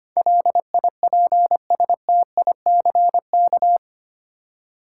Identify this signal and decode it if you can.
Morse code, heard as a keyed tone, about 25 wpm, 700 Hz. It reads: LIPSTICK
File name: practice / 25wpm / Easy Words with Hints - 25wpm